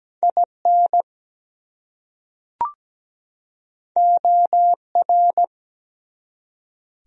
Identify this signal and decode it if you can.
Morse code, heard as a keyed tone, about 17 wpm, 700 Hz. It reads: IN OR
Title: Top 100 Words 17wpm